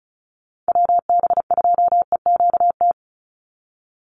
Morse code, heard as a keyed tone, about 36 wpm, 700 Hz. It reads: WB2EQT
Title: USA Call Signs 35wpm